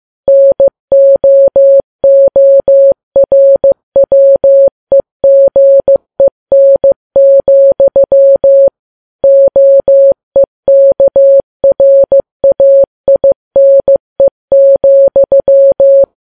Morse code, heard as a keyed tone, about 15 wpm, 550 Hz. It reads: NOORWEGEN, OEKRAINE,
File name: Files Web tls / Morse on9fv / morse15kps